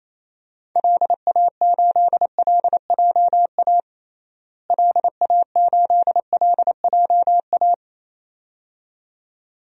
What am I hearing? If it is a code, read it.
Morse code, heard as a keyed tone, about 28 wpm, 700 Hz. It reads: LA8LJA LA8LJA
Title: DXCC-100 Call Signs - 6 Characters - Repeated 2x - 28wpm